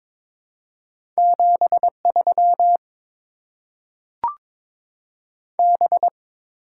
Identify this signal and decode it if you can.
Morse code, heard as a keyed tone, about 21 wpm, 700 Hz. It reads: 73 B
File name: Top 90 QSO Elements - 8x Character Spacing - 22wpm